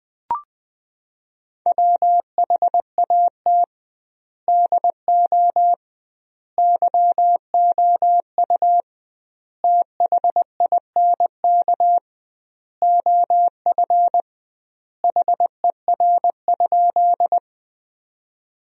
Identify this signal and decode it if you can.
Morse code, heard as a keyed tone, about 20 wpm, 700 Hz. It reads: WHAT DO YOU THINK OF HER?
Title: 2x Word-Spacing - Sentences from Top 200 Words - 20wpm